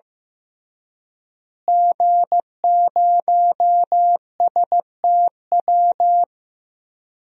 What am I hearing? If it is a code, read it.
Morse code, heard as a keyed tone, about 15 wpm, 700 Hz. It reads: G0STW
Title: DXCC-100 Call Signs - 5 Characters - Repeated 3x - 15wpm